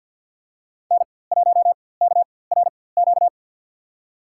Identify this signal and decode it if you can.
Morse code, heard as a keyed tone, about 48 wpm, spaced as for 23 wpm, 700 Hz. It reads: N1KRQ
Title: USA Call Signs - 4x Character Spacing - 50wpm